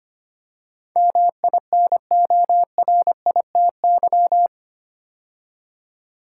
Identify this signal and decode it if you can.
Morse code, heard as a keyed tone, about 25 wpm, 700 Hz. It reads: MINORITY
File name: 8 Letter Words - First 3 Letters Given - 25wpm